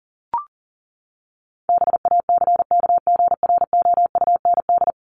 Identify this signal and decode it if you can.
Morse code, heard as a keyed tone, about 40 wpm, 700 Hz. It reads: BACKGROUND
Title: Top 1001-1250 Words 40wpm